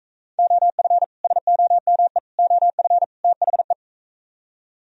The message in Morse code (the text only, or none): OF SOME OF THE